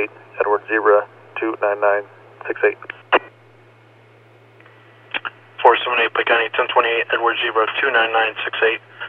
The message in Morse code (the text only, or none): none